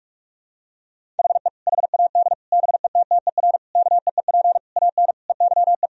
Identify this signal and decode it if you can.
Morse code, heard as a keyed tone, about 48 wpm, 700 Hz. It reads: HE HAD BETTER KEEP AN EYE
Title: Sentences from Top 1000 Words - Review Part 1 - 45wpm